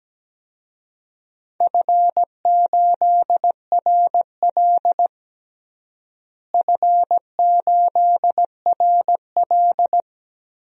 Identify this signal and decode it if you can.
Morse code, heard as a keyed tone, about 17 wpm, 700 Hz. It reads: F8RL F8RL
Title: DXCC-100 Call Signs - 4 Characters - Repeated 2x - 17wpm